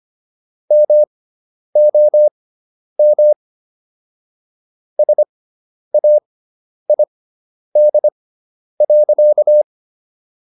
Morse code, heard as a keyed tone, about 25 wpm, spaced as for 10 wpm, 600 Hz. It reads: MOM SAID.